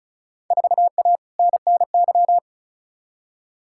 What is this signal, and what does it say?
Morse code, heard as a keyed tone, about 36 wpm, 700 Hz. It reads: 4A NNY
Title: Field Day Exchanges - 35wpm